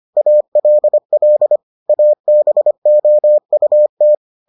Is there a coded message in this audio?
Yes: Morse code from 0.2 to 4.1 s, heard as a keyed tone, about 25 wpm, 600 Hz. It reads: ALL ABOUT